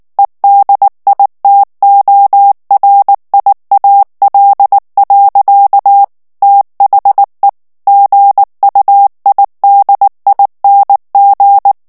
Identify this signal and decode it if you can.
Morse code, heard as a keyed tone, about 19 wpm, 800 Hz. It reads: EDITORIAL. THE GUIDING
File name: ANGLAIS texte clair 3 vitesse 18 mots minute